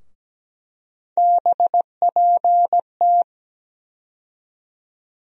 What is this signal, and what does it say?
Morse code, heard as a keyed tone, about 17 wpm, 700 Hz. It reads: BPT
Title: USA Call Sign Postfixes - 3 Letters 17wpm